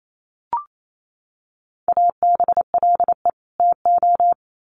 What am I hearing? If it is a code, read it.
Morse code, heard as a keyed tone, about 28 wpm, 700 Hz. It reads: ABLE TO
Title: Sets of 2 Words 28wpm